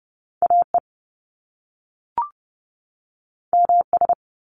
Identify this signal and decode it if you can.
Morse code, heard as a keyed tone, about 30 wpm, 700 Hz. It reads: AE MS
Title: Top Two Letter Combinations in Words 30wpm